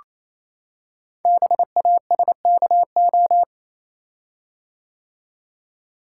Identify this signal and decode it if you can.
Morse code, heard as a keyed tone, about 28 wpm, 700 Hz. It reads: BASKO
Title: Code Groups - 5 Letters - 28wpm